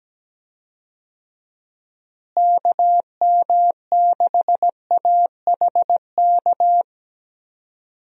Morse code, heard as a keyed tone, about 17 wpm, 700 Hz. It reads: KM6AHK
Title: USA Call Signs - 6 Characters - Repeated 3x - 17wpm